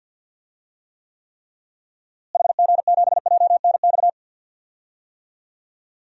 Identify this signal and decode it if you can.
Morse code, heard as a keyed tone, about 52 wpm, 700 Hz. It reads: HG7JNX
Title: DXCC-100 Call Signs - 6 Characters - Repeated 3x - 50wpm